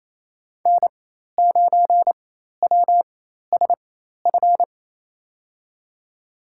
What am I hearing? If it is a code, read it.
Morse code, heard as a keyed tone, about 28 wpm, spaced as for 13 wpm, 700 Hz. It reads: N9WSF
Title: USA Call Signs - 5 Characters - 4x Character Spacing - 28wpm